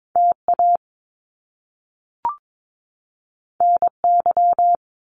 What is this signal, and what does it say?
Morse code, heard as a keyed tone, about 22 wpm, 700 Hz. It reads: T A N Y
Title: Top Two Letter Combinations in Words 22wpm